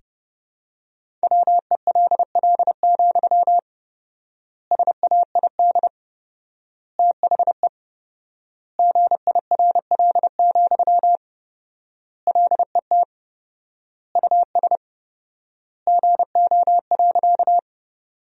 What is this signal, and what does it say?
Morse code, heard as a keyed tone, about 30 wpm, 700 Hz. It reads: WELL, SAID THE GIRL, LET US GO.
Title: 4x Word-Spacing - Sentences from Top 400 Words - 30wpm